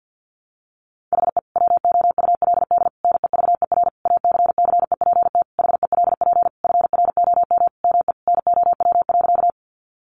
Morse code, heard as a keyed tone, about 48 wpm, 700 Hz. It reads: HE WOULD NEVER ACCEPT HELP FROM ME NOW.
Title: Sentences from Top 1300 Words 50wpm